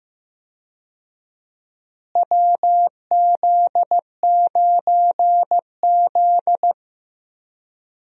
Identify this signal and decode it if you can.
Morse code, heard as a keyed tone, about 16 wpm, 700 Hz. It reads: WZ9Z
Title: USA Call Signs - 4 Characters - Repeated 3x - 15wpm